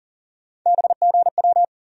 Morse code, heard as a keyed tone, about 40 wpm, 700 Hz. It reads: BGW
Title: USA Call Sign Postfixes - 3 Letters 40wpm